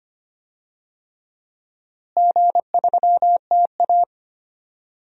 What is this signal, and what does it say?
Morse code, heard as a keyed tone, about 25 wpm, 700 Hz. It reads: G3TA
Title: DXCC-100 Call Signs - 4 Characters - Repeated 3x - 25wpm